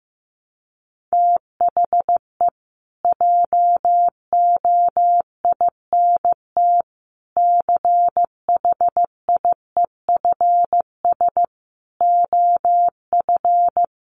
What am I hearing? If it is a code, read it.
Morse code, heard as a keyed tone, about 15 wpm, 700 Hz. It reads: THE JOINT CHIEFS OF
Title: Sets of 5 Words Encore2 15wpm